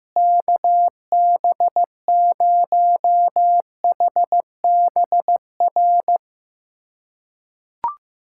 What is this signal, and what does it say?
Morse code, heard as a keyed tone, about 15 wpm, 700 Hz. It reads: KB0HBR
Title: USA Callsign DE Callsign - 6 Characters - 15wpm